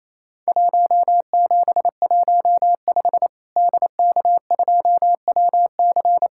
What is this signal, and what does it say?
Morse code, heard as a keyed tone, about 27 wpm, 700 Hz. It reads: 1715 DK2WC
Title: Worked All Europe DX Contest - QTC Exchange - 28wpm